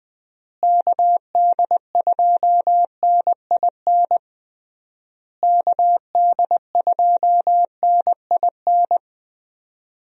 Morse code, heard as a keyed tone, about 20 wpm, 700 Hz. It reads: KD2NIN KD2NIN
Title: USA Call Signs - 6 Characters - Repeated 3x - 20wpm